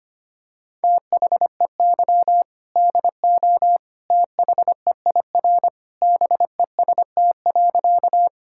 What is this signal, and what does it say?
Morse code, heard as a keyed tone, about 25 wpm, 700 Hz. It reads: THEY DO THEIR BEST.
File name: practice / 25wpm / Sentences with 2-Word Phrases - 25wpm